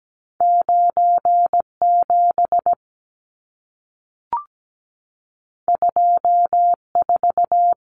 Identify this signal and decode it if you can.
Morse code, heard as a keyed tone, about 17 wpm, 700 Hz. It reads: 97 24
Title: Numbers - 2-Digit 17wpm